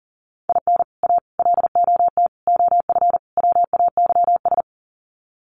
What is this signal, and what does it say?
Morse code, heard as a keyed tone, about 40 wpm, 700 Hz. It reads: IN A LOT OF WAYS
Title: Sets of 5 Words Encore 40wpm